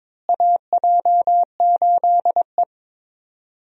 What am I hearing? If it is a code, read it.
Morse code, heard as a keyed tone, about 22 wpm, 700 Hz. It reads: AJ8E
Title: USA Call Signs - 4 Characters - Repeated 2x - 22wpm